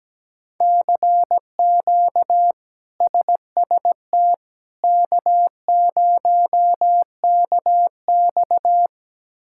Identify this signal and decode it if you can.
Morse code, heard as a keyed tone, about 17 wpm, 700 Hz. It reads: CQ SST K0KX
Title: K1USN SST QSO Practice - 17wpm